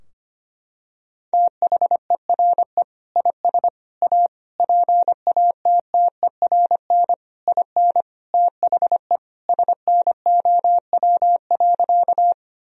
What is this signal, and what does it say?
Morse code, heard as a keyed tone, about 25 wpm, 700 Hz. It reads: THERE IS A PATTERN IN THE SNOW.
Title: Sentences from Top 1000 Words - Review Part 2 - 25wpm